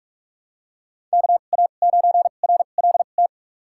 Morse code, heard as a keyed tone, about 46 wpm, spaced as for 33 wpm, 700 Hz. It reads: KA9RLT
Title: USA Call Signs - 6 Characters - 2x Character Spacing - 45wpm